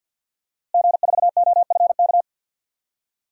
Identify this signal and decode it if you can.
Morse code, heard as a keyed tone, about 52 wpm, 700 Hz. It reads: G4OFK
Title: DXCC-100 Call Signs - 5 Characters - Repeated 3x - 50wpm